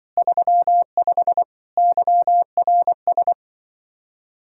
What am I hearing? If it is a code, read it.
Morse code, heard as a keyed tone, about 23 wpm, 700 Hz. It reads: 35 YRS
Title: Speed-Racing Common QSO Phrases - 20wpm